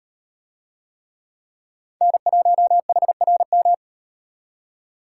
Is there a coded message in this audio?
Yes: Morse code from 2.0 to 3.7 s, heard as a keyed tone, about 38 wpm, 700 Hz. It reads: N1HRM